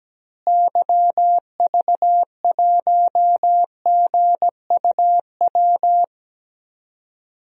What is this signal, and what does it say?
Morse code, heard as a keyed tone, about 17 wpm, 700 Hz. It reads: YV1GUW
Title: DXCC-100 Call Signs - 6 Characters - Repeated 3x - 17wpm